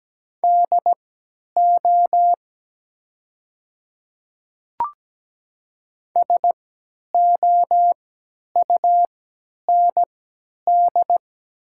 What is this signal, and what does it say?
Morse code, heard as a keyed tone, about 17 wpm, spaced as for 10 wpm, 700 Hz. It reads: DO SOUND
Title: 3x Character Spacing - Top 100 Words - 17wpm